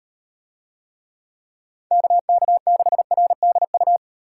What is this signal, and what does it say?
Morse code, heard as a keyed tone, about 38 wpm, 700 Hz. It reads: KK6RDU